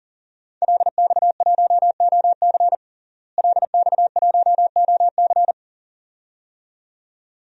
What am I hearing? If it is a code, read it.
Morse code, heard as a keyed tone, about 40 wpm, 700 Hz. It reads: LX1OC LX1OC
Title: DXCC-100 Call Signs - 5 Characters - Repeated 2x - 40wpm